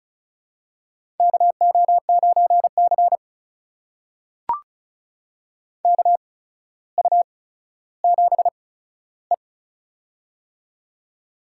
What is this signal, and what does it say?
Morse code, heard as a keyed tone, about 36 wpm, spaced as for 15 wpm, 700 Hz. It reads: KO9C KU7E
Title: USA Call Signs - 4 Characters - 8x Character Spacing - 35wpm